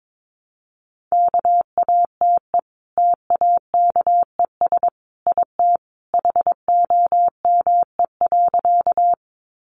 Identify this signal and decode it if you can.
Morse code, heard as a keyed tone, about 22 wpm, 700 Hz. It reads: KATE TAKES IT HOME.
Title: Easy Phrases 22wpm